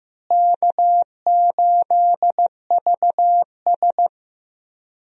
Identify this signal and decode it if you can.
Morse code, heard as a keyed tone, about 15 wpm, 700 Hz. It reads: K8VS
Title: USA Call Signs - 4 Characters - Repeated 3x - 15wpm